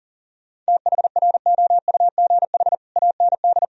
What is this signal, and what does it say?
Morse code, heard as a keyed tone, about 37 wpm, 700 Hz. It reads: THROUGH AND
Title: Sentences from Top 1000 Words - Review Part 1 - 40wpm